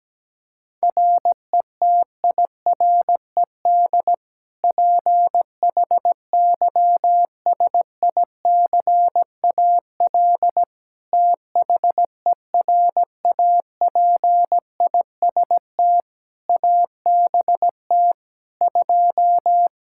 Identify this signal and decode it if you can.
Morse code, heard as a keyed tone, about 17 wpm, 700 Hz. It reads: RETIRED PHYSICAL THERAPIST ABT 2
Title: K1USN SST QSO Practice 17wpm